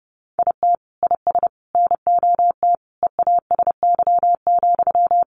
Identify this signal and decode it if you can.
Morse code, heard as a keyed tone, about 30 wpm, 700 Hz. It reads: IT IS NOT EASY,
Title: Sentences from Top 700 Words 30wpm